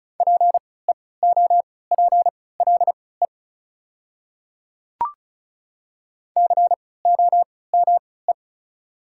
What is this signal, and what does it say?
Morse code, heard as a keyed tone, about 36 wpm, spaced as for 20 wpm, 700 Hz. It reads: PEOPLE COME
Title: 3x Character Spacing - Top 300 Words - 35wpm